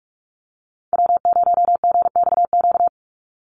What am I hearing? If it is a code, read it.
Morse code, heard as a keyed tone, about 46 wpm, 700 Hz. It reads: W0GXQ
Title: Call Signs - 45wpm